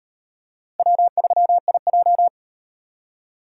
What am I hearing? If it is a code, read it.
Morse code, heard as a keyed tone, about 38 wpm, 700 Hz. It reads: W3IJ